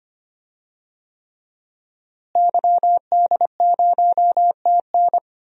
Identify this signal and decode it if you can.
Morse code, heard as a keyed tone, about 25 wpm, 700 Hz. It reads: YD0TN
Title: DXCC-100 Call Signs - 5 Characters - Repeated 3x - 25wpm